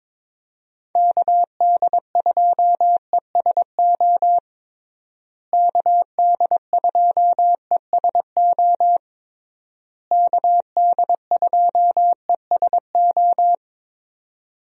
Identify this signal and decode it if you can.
Morse code, heard as a keyed tone, about 22 wpm, 700 Hz. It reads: KD2ESO KD2ESO KD2ESO
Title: USA Call Signs - 6 Characters - Repeated 3x - 22wpm